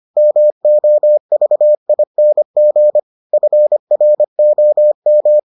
Morse code, heard as a keyed tone, about 25 wpm, 600 Hz. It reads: MOVING FROM